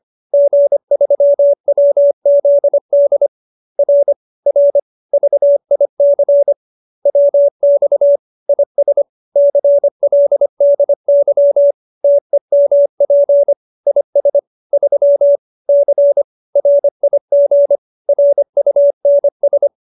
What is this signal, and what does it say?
Morse code, heard as a keyed tone, about 25 wpm, 570 Hz. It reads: G3WZD R R VIC WX IS CLDY TEMP IS 3 C RIG RUNS
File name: QSO310 25